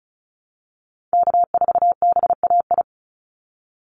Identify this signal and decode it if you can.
Morse code, heard as a keyed tone, about 34 wpm, 700 Hz. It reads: K4BAI
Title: Call Signs Repeated 3x - 35wpm